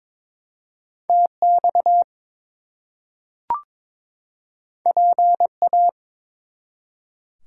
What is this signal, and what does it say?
Morse code, heard as a keyed tone, about 22 wpm, 700 Hz. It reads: TX PA
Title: US States and Canadian Provinces 22wpm